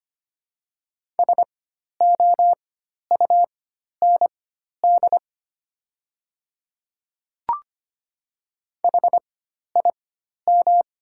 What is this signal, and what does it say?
Morse code, heard as a keyed tone, about 25 wpm, spaced as for 12 wpm, 700 Hz. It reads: SOUND HIM
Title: 4x Character Spacing - Top 100 Words - 25wpm